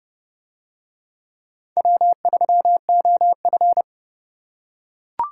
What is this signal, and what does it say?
Morse code, heard as a keyed tone, about 28 wpm, 700 Hz. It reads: W3OF